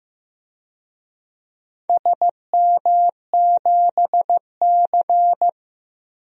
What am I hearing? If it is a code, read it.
Morse code, heard as a keyed tone, about 15 wpm, 700 Hz. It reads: SM7C
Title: DXCC-100 Call Signs - 4 Characters - Repeated 3x - 15wpm